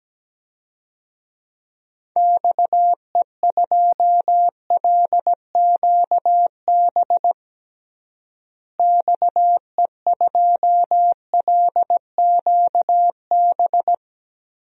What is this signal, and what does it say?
Morse code, heard as a keyed tone, about 17 wpm, 700 Hz. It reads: XE2LQB XE2LQB
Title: DXCC-100 Call Signs - 6 Characters - Repeated 3x - 17wpm